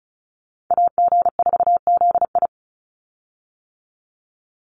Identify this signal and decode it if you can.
Morse code, heard as a keyed tone, about 34 wpm, 700 Hz. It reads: AG4ZI
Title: USA Call Signs 35wpm